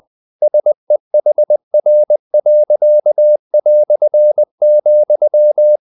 Morse code, heard as a keyed tone, about 20 wpm, 600 Hz. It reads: SEHR.",
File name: dk5ke / Humor-100BpM